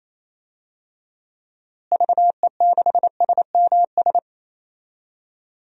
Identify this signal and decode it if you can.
Morse code, heard as a keyed tone, about 28 wpm, 700 Hz. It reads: VE6SMS
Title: DXCC-100 Call Signs - 6 Characters - Repeated 2x - 28wpm